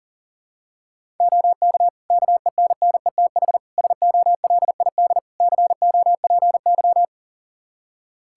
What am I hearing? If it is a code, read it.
Morse code, heard as a keyed tone, about 40 wpm, 700 Hz. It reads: OK KENNETH SOLID COPY
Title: Practice QSO - Exchange 3 - More Info - 40wpm